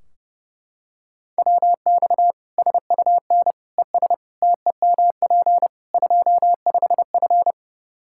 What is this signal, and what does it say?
Morse code, heard as a keyed tone, about 30 wpm, 700 Hz. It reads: WX SUN ES TEMP 25F
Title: Practice QSO - Exchange 2 - Info - 30wpm